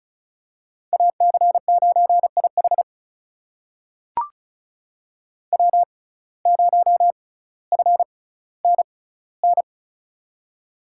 Morse code, heard as a keyed tone, about 36 wpm, spaced as for 18 wpm, 700 Hz. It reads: AC9IH W0FNN
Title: USA Call Signs - 5 Characters - 6x Character Spacing - 35wpm